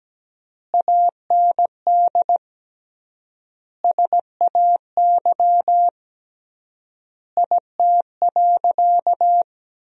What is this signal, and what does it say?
Morse code, heard as a keyed tone, about 17 wpm, 700 Hz. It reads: AND SAY IT.
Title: 3x Word-Spacing - Sentences from Top 400 Words - 17wpm